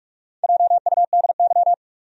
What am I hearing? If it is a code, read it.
Morse code, heard as a keyed tone, about 44 wpm, 700 Hz. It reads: JUDY
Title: USA Common Names - Max 5 Letters - 45wpm